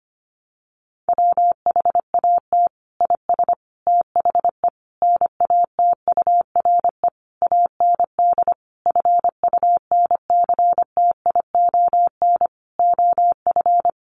Morse code, heard as a keyed tone, about 25 wpm, 700 Hz. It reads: WHAT IS THE NATURE AND FUNCTION OF
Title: Sentences from Top 1100 Words 25wpm